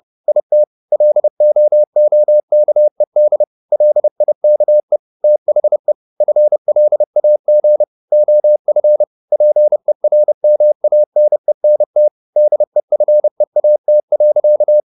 Morse code, heard as a keyed tone, about 30 wpm, 600 Hz. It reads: IT LOOKED LIKE THE FLAG OF PERMANENT DEFEAT.
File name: the old man and the sea / the old man and the sea 1 30wpm